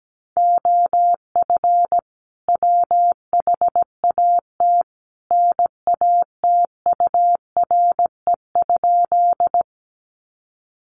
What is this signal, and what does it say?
Morse code, heard as a keyed tone, about 17 wpm, 700 Hz. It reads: OF WHAT NATURE?
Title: Sentences from Top 600 Words 17wpm